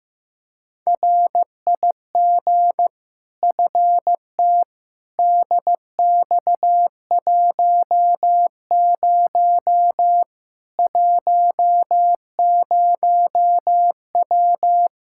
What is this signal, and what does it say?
Morse code, heard as a keyed tone, about 15 wpm, 700 Hz. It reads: RIG FT DX10 10W
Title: Practice QSO - Exchange 2 - Info - 15wpm